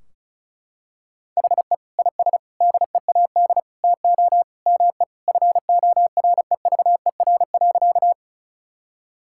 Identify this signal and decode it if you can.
Morse code, heard as a keyed tone, about 36 wpm, 700 Hz. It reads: HE IS DEAD TO ME FOREVER.
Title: Sentences from Top 1600 Words 35wpm